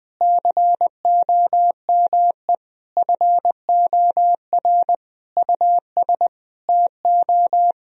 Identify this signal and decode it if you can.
Morse code, heard as a keyed tone, about 20 wpm, 700 Hz. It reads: COME FOR US TO
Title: Sentences from Top 1000 Words - Review Part 1 - 20wpm